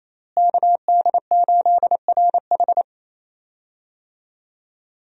Morse code, heard as a keyed tone, about 28 wpm, 700 Hz. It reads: KD8RH
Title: USA Call Signs - 5 Characters - Repeated 2x - 28wpm